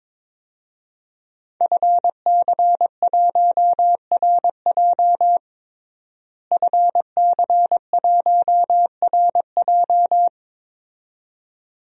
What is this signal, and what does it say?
Morse code, heard as a keyed tone, about 22 wpm, 700 Hz. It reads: FC1RJ FC1RJ
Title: DXCC-100 Call Signs - 5 Characters - Repeated 2x - 22wpm